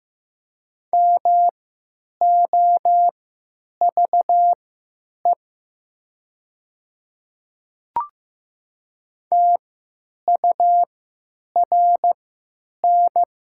Morse code, heard as a keyed tone, about 15 wpm, spaced as for 9 wpm, 700 Hz. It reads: MOVE TURN
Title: 3x Character Spacing - Top 200 Words - 15wpm